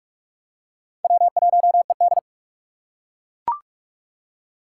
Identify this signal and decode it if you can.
Morse code, heard as a keyed tone, about 41 wpm, 700 Hz. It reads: W1ED
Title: USA Call Signs - 4 Characters - 4x Character Spacing - 45wpm